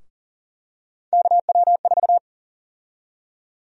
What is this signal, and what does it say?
Morse code, heard as a keyed tone, about 40 wpm, 700 Hz. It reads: KW4
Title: USA Call Sign Prefixes - Repeated 2x - 40wpm